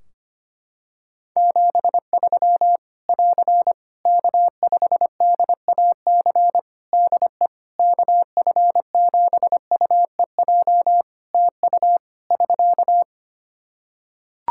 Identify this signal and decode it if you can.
Morse code, heard as a keyed tone, about 25 wpm, 700 Hz. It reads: 73 <AR> K5DAC DE KF7UEJ TU <SK>
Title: Practice QSO - Exchange 4 - Ending - 25wpm